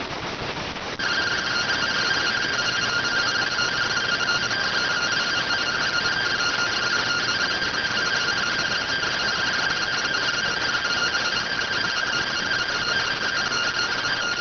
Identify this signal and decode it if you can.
Two-tone teleprinter signal, about 170 Hz shift.